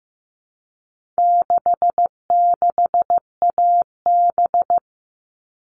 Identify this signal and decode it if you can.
Morse code, heard as a keyed tone, about 15 wpm, 700 Hz. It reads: 66AB
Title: All Cut Numbers - 4-Digit 15wpm